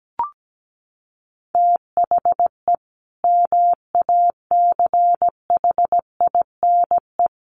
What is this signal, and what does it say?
Morse code, heard as a keyed tone, about 17 wpm, 700 Hz. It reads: THE MACHINE
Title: A Sound Of Thunder 17wpm